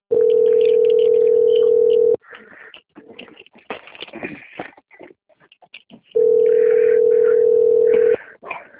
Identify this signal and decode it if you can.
Ringback tone (North America), 440 + 480 Hz.
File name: out- - - - - .15016